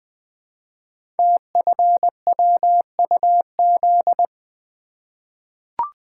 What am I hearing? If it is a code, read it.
Morse code, heard as a keyed tone, about 20 wpm, 700 Hz. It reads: TFWUZ
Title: Code Groups - 5 Letter:Numbers - 20wpm